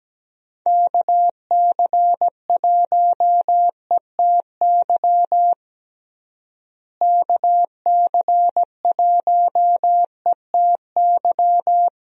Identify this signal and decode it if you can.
Morse code, heard as a keyed tone, about 17 wpm, 700 Hz. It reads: KC1ETY KC1ETY
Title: USA Call Signs - 6 Characters - Repeated 3x - 17wpm